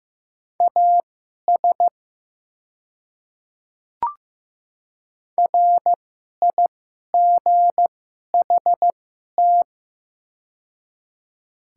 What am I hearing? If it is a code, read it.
Morse code, heard as a keyed tone, about 15 wpm, spaced as for 11 wpm, 700 Hz. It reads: AS RIGHT
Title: 2x Character Spacing - Top 200 Words - 15wpm